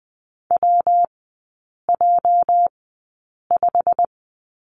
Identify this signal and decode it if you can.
Morse code, heard as a keyed tone, about 19 wpm, spaced as for 8 wpm, 700 Hz. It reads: WJ5
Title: 205 Lesson 33e - Letter j - Three Characters 20wpm